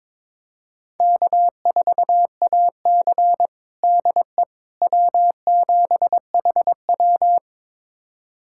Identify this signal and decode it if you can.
Morse code, heard as a keyed tone, about 22 wpm, 700 Hz. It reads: K4AC DE W7HW
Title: USA Callsign DE Callsign - 4 Characters - 22wpm